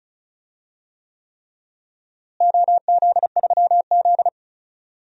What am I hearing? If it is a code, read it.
Morse code, heard as a keyed tone, about 34 wpm, 700 Hz. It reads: OZ3Z
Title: DXCC-100 Call Signs - 4 Characters - Repeated 3x - 35wpm